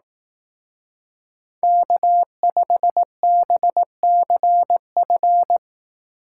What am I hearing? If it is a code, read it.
Morse code, heard as a keyed tone, about 18 wpm, 700 Hz. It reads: K5BCF
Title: Speed Racing - USA Call Signs 15wpm